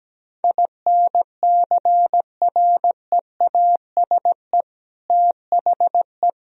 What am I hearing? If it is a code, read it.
Morse code, heard as a keyed tone, about 17 wpm, 700 Hz. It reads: INCREASE THE
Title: Sentences from Top 1000 Words - Review Part 4 - 17wpm